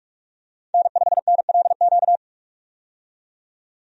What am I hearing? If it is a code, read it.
Morse code, heard as a keyed tone, about 46 wpm, 700 Hz. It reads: N5NLQ
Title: USA Call Signs - 5 Characters - 45wpm